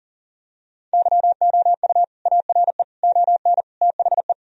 Morse code, heard as a keyed tone, about 40 wpm, 700 Hz. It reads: YOU ARE ON THE
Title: Sentences from Top 1000 Words - Review Part 1 - 40wpm